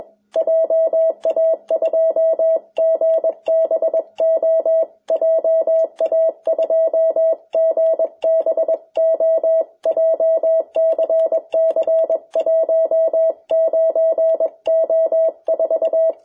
Morse code, heard as a keyed tone, about 21 wpm, 650 Hz. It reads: JA2GBO JA2GBO JCC 19O4
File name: jh8cbh / 2021cw